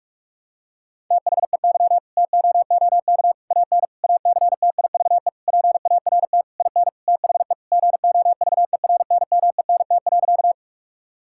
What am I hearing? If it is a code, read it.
Morse code, heard as a keyed tone, about 46 wpm, 700 Hz. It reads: THEY TOOK AN ACTIVE PART IN THE GOVERNMENT.
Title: Sentences from Top 1000 Words - Review Part 6 - 45wpm